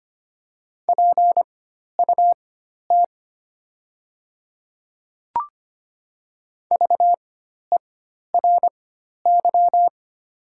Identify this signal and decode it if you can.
Morse code, heard as a keyed tone, about 25 wpm, spaced as for 12 wpm, 690 Hz. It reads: PUT VERY
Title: 4x Character Spacing - Top 200 Words - 25wpm